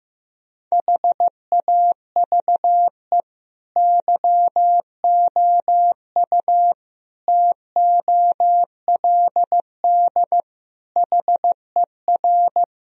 Morse code, heard as a keyed tone, about 15 wpm, 700 Hz. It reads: HAVE YOU TOLD HER
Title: Sentences from Top 1000 Words - Review Part 2 - 15wpm